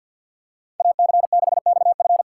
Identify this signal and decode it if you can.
Morse code, heard as a keyed tone, about 51 wpm, 700 Hz. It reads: AC6XF
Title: USA Call Signs - 3x Character Spacing - 50wpm